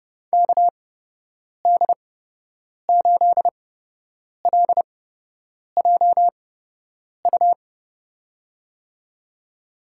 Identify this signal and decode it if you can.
Morse code, heard as a keyed tone, about 30 wpm, spaced as for 8 wpm, 700 Hz. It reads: KD8LJU
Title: USA Call Signs - 8x Character Spacing - 30wpm